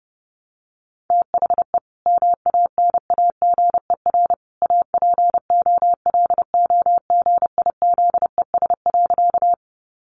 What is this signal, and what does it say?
Morse code, heard as a keyed tone, about 30 wpm, 700 Hz. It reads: THE MANAGER APOLOGIZES.
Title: Additional Easy Phrases 30wpm